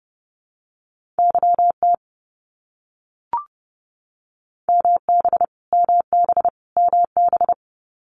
Morse code, heard as a keyed tone, about 30 wpm, 700 Hz. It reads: YT MB MB MB
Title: Canadian Provinces and Territories - Repeated 3x - 30wpm